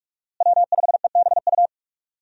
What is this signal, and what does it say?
Morse code, heard as a keyed tone, about 44 wpm, 700 Hz. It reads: W5EBU
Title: USA Call Signs - 5 Characters - Repeated 2x - 45wpm